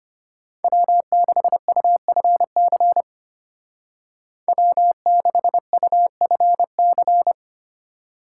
Morse code, heard as a keyed tone, about 33 wpm, 700 Hz. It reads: W6UFC W6UFC